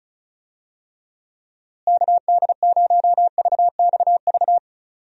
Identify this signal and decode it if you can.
Morse code, heard as a keyed tone, about 34 wpm, 700 Hz. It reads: KD0VXV
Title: USA Call Signs - 6 Characters - Repeated 3x - 35wpm